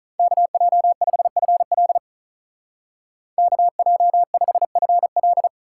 Morse code, heard as a keyed tone, about 44 wpm, 700 Hz. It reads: KJ5FL KJ5FL